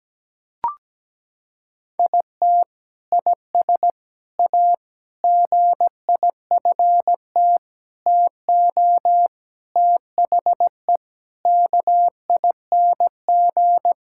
Morse code, heard as a keyed tone, about 17 wpm, 700 Hz. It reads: IT IS A GIFT TO THE KING
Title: Sentences from Top 1000 Words - Review Part 6 - 17wpm